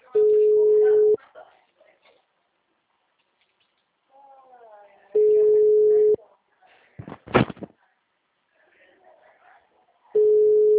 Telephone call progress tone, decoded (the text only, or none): ringback tone